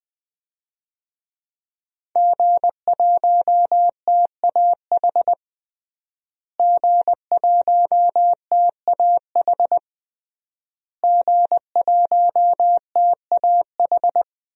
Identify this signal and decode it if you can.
Morse code, heard as a keyed tone, about 20 wpm, 700 Hz. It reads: G1TAH G1TAH G1TAH
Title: DXCC-100 Call Signs - 5 Characters - Repeated 3x - 20wpm